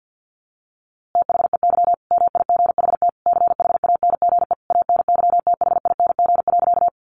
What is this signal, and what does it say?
Morse code, heard as a keyed tone, about 50 wpm, 700 Hz. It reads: THEY MIGHT CHANGE ANYTHING.
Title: A Sound Of Thunder 50wpm